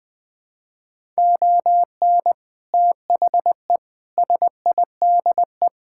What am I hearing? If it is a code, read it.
Morse code, heard as a keyed tone, about 20 wpm, 700 Hz. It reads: ON THE SIDE
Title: Sentences with 3-Word Phrases - 20wpm